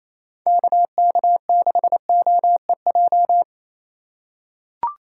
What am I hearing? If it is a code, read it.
Morse code, heard as a keyed tone, about 28 wpm, 700 Hz. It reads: KK6OEJ
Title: USA Call Signs - 6 Characters - 4x Character Spacing - 28wpm